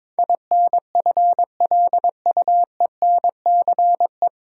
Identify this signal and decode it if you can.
Morse code, heard as a keyed tone, about 23 wpm, 700 Hz. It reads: INFLUENCE
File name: 9 Letter Words - First 3 Letters Given - 22wpm